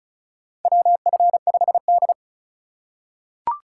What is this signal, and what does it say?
Morse code, heard as a keyed tone, about 34 wpm, 700 Hz. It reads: WF5D
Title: USA Call Signs - 2x Character Spacing - 35wpm